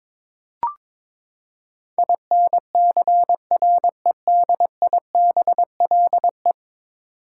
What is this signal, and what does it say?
Morse code, heard as a keyed tone, about 21 wpm, 700 Hz. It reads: INCREDIBLE
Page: morse-code-ninja-practice-22wpm